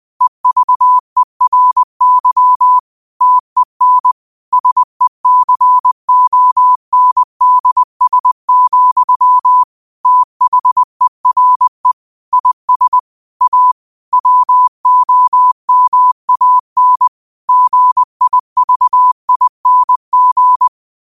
Morse code, heard as a keyed tone, about 20 wpm, 1000 Hz. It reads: EVERY TEN SECONDS, THERE IS A WOMAN GIVING